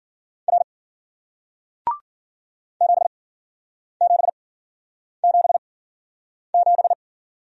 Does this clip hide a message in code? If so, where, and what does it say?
Yes: Morse code from 0.5 to 6.9 s, heard as a keyed tone, about 54 wpm, spaced as for 9 wpm, 700 Hz. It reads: R 7777